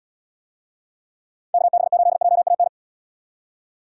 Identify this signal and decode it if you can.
Morse code, heard as a keyed tone, about 76 wpm, 700 Hz. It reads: KB8JAN